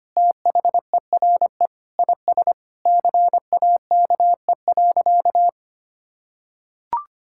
Morse code, heard as a keyed tone, about 25 wpm, 690 Hz. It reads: THERE IS CAKE.
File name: practice / 25wpm / Sentences with 2-Word Phrases - 25wpm